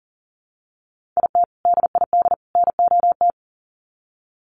Morse code, heard as a keyed tone, about 40 wpm, 700 Hz. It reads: IT DID NOT
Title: Sets of 3 Words 40wpm